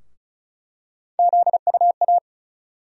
Morse code, heard as a keyed tone, about 34 wpm, 700 Hz. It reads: ZUA
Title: USA Call Sign Postfixes - 3 Letters 35wpm